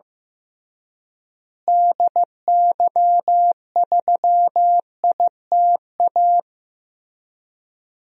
Morse code, heard as a keyed tone, about 15 wpm, 700 Hz. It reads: DY3ITA
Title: DXCC-100 Call Signs - 6 Characters - Repeated 3x - 15wpm